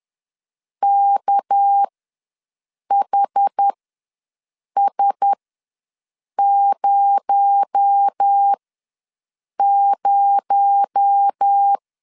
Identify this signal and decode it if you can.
Morse code, heard as a keyed tone, about 10 wpm, spaced as for 6 wpm, 800 Hz. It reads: KHS00
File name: LEZIONI / 6 PKX0 / 6 PKX0 ws 10wpm 4220